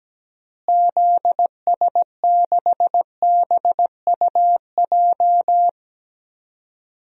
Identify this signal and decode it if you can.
Morse code, heard as a keyed tone, about 17 wpm, 700 Hz. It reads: ZS6BUJ
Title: DXCC-100 Call Signs - 6 Characters - Repeated 3x - 17wpm